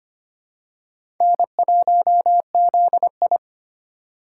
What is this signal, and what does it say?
Morse code, heard as a keyed tone, about 25 wpm, 700 Hz. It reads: N1ZI
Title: USA Call Signs - 4x Character Spacing - 25wpm